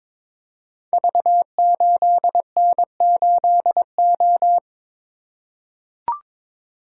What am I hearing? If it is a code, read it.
Morse code, heard as a keyed tone, about 22 wpm, 700 Hz. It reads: V8N8O
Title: Code Groups - 5 Letter:Numbers - 22wpm